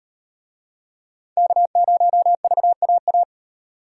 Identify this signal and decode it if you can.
Morse code, heard as a keyed tone, about 37 wpm, 700 Hz. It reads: K0VAA